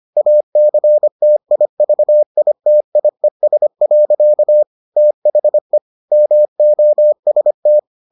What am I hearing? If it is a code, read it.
Morse code, heard as a keyed tone, about 25 wpm, 600 Hz. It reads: ACTIVITIES. THE MOST